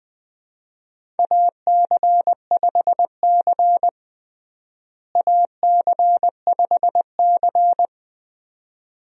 Morse code, heard as a keyed tone, about 20 wpm, 700 Hz. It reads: AC5C AC5C
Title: USA Call Signs - 4 Characters - Repeated 2x - 20wpm